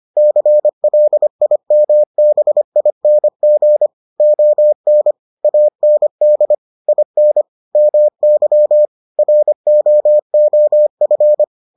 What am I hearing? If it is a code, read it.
Morse code, heard as a keyed tone, about 25 wpm, 600 Hz. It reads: CLIMBING ON AND IN MY ROOF